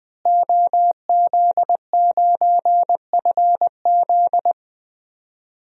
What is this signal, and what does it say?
Morse code, heard as a keyed tone, about 20 wpm, 700 Hz. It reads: OZ9FZ
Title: DXCC-100 Call Signs - 5 Characters - Repeated 3x - 20wpm